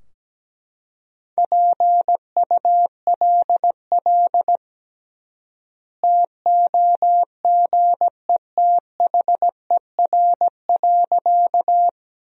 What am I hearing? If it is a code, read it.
Morse code, heard as a keyed tone, about 17 wpm, 700 Hz. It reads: PULL TOGETHER.
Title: 3x Word-Spacing - Sentences from Top 400 Words - 17wpm